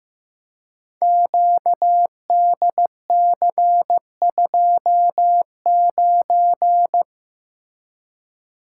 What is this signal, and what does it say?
Morse code, heard as a keyed tone, about 15 wpm, 700 Hz. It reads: QDC29
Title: Code Groups - 5 Letter:Numbers - 15wpm